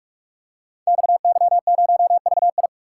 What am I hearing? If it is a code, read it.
Morse code, heard as a keyed tone, about 44 wpm, 700 Hz. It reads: XY0VI
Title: Code Groups - 5 Letter:Numbers - 45wpm